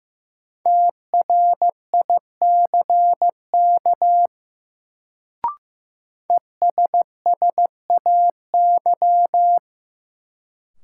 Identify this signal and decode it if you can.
Morse code, heard as a keyed tone, about 15 wpm, 700 Hz. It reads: TRICK ESSAY
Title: Sentences from Top 1500 Words - New Word Drill Alt - 15wpm